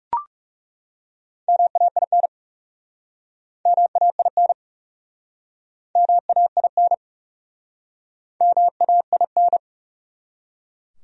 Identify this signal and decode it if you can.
Morse code, heard as a keyed tone, about 37 wpm, 700 Hz. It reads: MAIN MAIN MAIN MAIN
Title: Speed-Racing Top 500 Words 30wpm